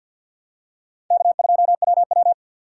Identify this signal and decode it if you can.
Morse code, heard as a keyed tone, about 52 wpm, 700 Hz. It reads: K2WW
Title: USA Call Signs - 4 Characters - 3x Character Spacing - 50wpm